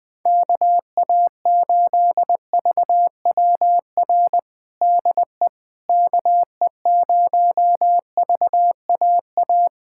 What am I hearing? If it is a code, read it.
Morse code, heard as a keyed tone, about 20 wpm, 700 Hz. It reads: KA8VWR DE KE0VAA
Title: USA Callsign DE Callsign - 6 Characters - 20wpm